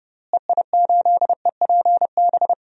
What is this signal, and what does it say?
Morse code, heard as a keyed tone, about 30 wpm, 700 Hz. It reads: EI8EPB
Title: DXCC-100 Call Signs - 6 Characters - Repeated 2x - 30wpm